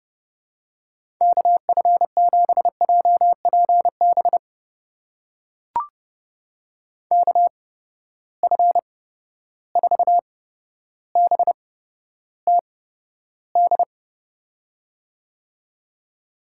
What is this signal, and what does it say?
Morse code, heard as a keyed tone, about 30 wpm, spaced as for 13 wpm, 700 Hz. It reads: KF7JPB KF4BTD
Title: USA Call Signs - 6 Characters - 8x Character Spacing - 30wpm